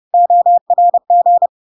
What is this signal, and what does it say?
Morse code, heard as a keyed tone, about 30 wpm, 700 Hz. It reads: ORG
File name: LARK-FD-2022-Log-in-CW-30wpm-700-Hz